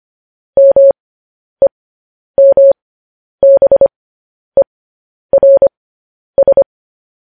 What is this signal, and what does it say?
Morse code, heard as a keyed tone, about 25 wpm, spaced as for 10 wpm, 550 Hz. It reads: MEMBERS
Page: News Headlines in Morse Code at 10 WPM